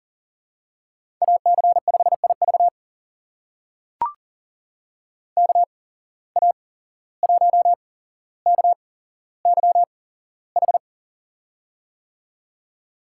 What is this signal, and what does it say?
Morse code, heard as a keyed tone, about 39 wpm, spaced as for 16 wpm, 700 Hz. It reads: AC5IV KA1KYH
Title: USA Call Signs - 8x Character Spacing - 40wpm